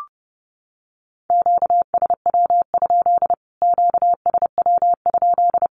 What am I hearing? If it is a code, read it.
Morse code, heard as a keyed tone, about 30 wpm, 700 Hz. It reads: QSW? QSW?
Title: Q-Codes Repeated 3x 30wpm